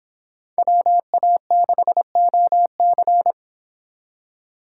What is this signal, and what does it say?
Morse code, heard as a keyed tone, about 26 wpm, 700 Hz. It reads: WA6OC